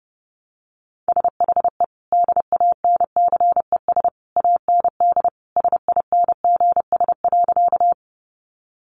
Morse code, heard as a keyed tone, about 30 wpm, 700 Hz. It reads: SHE DANCES AND SINGS.
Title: Easy Phrases 30wpm